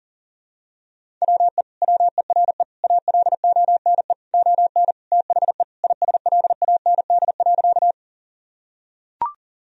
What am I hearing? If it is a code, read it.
Morse code, heard as a keyed tone, about 40 wpm, 700 Hz. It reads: WE WERE ALONE ON THE ISLAND.
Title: Sentences from Top 1000 Words - Review Part 3 - 40wpm